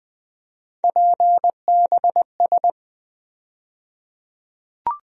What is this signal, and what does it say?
Morse code, heard as a keyed tone, about 20 wpm, 700 Hz. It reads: PBS
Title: USA Call Sign Postfixes - 3 Letters 20wpm